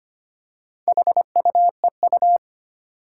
Morse code, heard as a keyed tone, about 25 wpm, 700 Hz. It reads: HUEU
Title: Four Challenging Letter-Number Combinations 25wpm